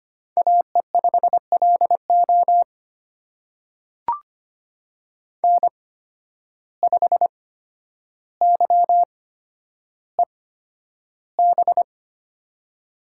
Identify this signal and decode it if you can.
Morse code, heard as a keyed tone, about 25 wpm, spaced as for 11 wpm, 700 Hz. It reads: AE5LO N5YEB
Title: USA Call Signs - 5 Characters - 8x Character Spacing - 25wpm